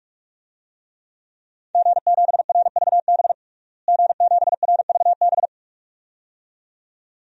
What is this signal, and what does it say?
Morse code, heard as a keyed tone, about 46 wpm, 700 Hz. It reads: G7RVB G7RVB
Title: DXCC-100 Call Signs - 5 Characters - Repeated 2x - 45wpm